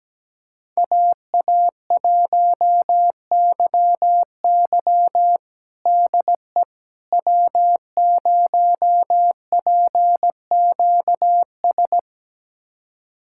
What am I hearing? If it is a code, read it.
Morse code, heard as a keyed tone, about 17 wpm, 700 Hz. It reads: AA1YY DE W0PQS
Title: USA Callsign DE Callsign - 5 Characters - 17wpm